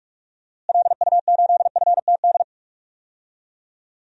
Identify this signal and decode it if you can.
Morse code, heard as a keyed tone, about 46 wpm, 700 Hz. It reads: LU8FTD
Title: DXCC-100 Call Signs - 6 Characters - Repeated 2x - 45wpm